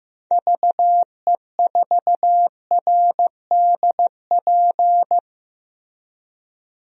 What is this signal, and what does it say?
Morse code, heard as a keyed tone, about 15 wpm, 700 Hz. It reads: VE4RDP
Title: DXCC-100 Call Signs - 6 Characters - 15wpm